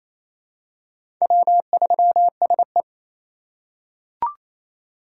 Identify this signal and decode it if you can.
Morse code, heard as a keyed tone, about 28 wpm, 700 Hz. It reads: W3SE
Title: USA Call Signs - 4 Characters - 4x Character Spacing - 28wpm